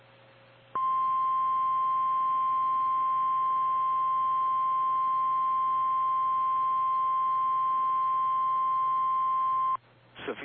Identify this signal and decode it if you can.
Emergency Alert System (SAME) tones (the 1050 Hz attention tone). It is heard from 0.7 to 9.8 s.
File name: eas / KKPRKXPN / KKPRKXPN